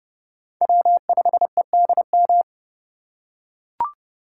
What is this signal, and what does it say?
Morse code, heard as a keyed tone, about 30 wpm, 700 Hz. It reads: W5EDM
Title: USA Call Signs - 5 Characters - 2x Character Spacing - 30wpm